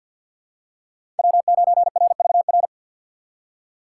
Morse code, heard as a keyed tone, about 52 wpm, 700 Hz. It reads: W9RVR
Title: USA Call Signs - 5 Characters - 6x Character Spacing - 50wpm